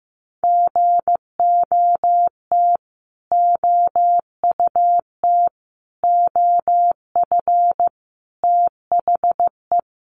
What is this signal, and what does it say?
Morse code, heard as a keyed tone, about 15 wpm, 700 Hz. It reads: GOT OUT OF THE
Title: Sets of 5 Words Encore 15wpm